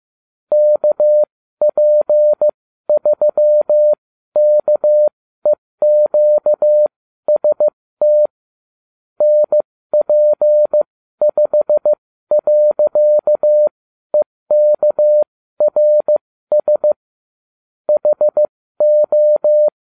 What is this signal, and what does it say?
Morse code, heard as a keyed tone, about 15 wpm, spaced as for 12 wpm, 600 Hz. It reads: KP3KEQST NP5.EKRS HO